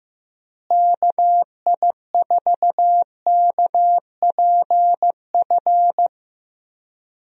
Morse code, heard as a keyed tone, about 15 wpm, 700 Hz. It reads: KI4KPF
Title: USA Call Signs - 6 Characters - Repeated 3x - 15wpm